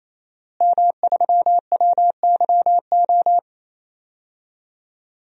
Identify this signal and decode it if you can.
Morse code, heard as a keyed tone, about 27 wpm, 700 Hz. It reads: M3WYO
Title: DXCC-100 Call Signs - 5 Characters - Repeated 3x - 28wpm